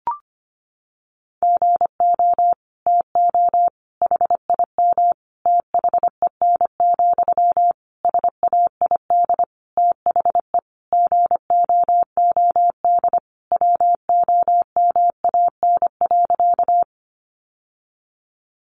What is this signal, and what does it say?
Morse code, heard as a keyed tone, about 25 wpm, 700 Hz. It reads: GO TO HIM THEN, SAID THE GOOD WOMAN.
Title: Sentences from Top 1100 Words 25wpm